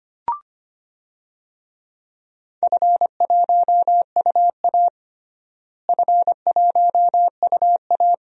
Morse code, heard as a keyed tone, about 25 wpm, 700 Hz. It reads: F1UA F1UA
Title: DXCC-100 Call Signs - 4 Characters - Repeated 2x - 25wpm